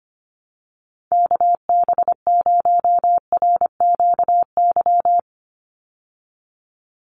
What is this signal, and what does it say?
Morse code, heard as a keyed tone, about 25 wpm, 700 Hz. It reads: KB0RQY
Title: USA Call Signs 25wpm